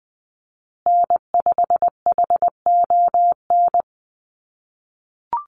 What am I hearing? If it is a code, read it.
Morse code, heard as a keyed tone, about 20 wpm, 700 Hz. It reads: N5HON
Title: USA Call Signs Repeated 2x 20wpm